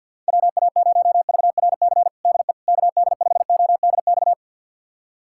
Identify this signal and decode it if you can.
Morse code, heard as a keyed tone, about 47 wpm, 700 Hz. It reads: WA0VRC DE KD5ODX
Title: USA Callsign DE Callsign - 6 Characters - 50wpm